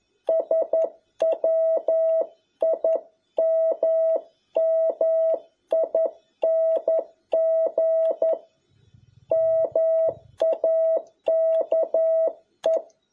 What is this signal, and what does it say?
Morse code, heard as a keyed tone, about 11 wpm, 650 Hz. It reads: SWIMMING MAKE
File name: jh8cbh / 2021cw